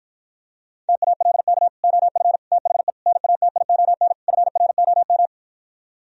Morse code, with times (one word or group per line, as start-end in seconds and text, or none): 0.9-1.7 TALK
1.8-2.4 OF
2.5-2.9 THE
3.1-4.1 NATION
4.3-5.3 FROM